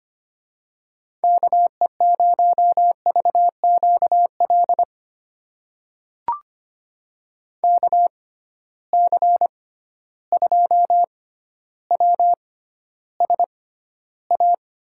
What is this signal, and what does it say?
Morse code, heard as a keyed tone, about 25 wpm, spaced as for 13 wpm, 700 Hz. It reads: KE0VQL KC2WSA
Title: USA Call Signs - 6 Characters - 6x Character Spacing - 25wpm